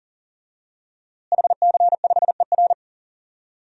Morse code, heard as a keyed tone, about 38 wpm, 700 Hz. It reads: HC5ER
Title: DXCC-100 Call Signs - 5 Characters - Repeated 2x - 40wpm